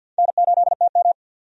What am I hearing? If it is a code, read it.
Morse code, heard as a keyed tone, about 47 wpm, 700 Hz. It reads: N8TM
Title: USA Call Signs - 4 Characters - 50wpm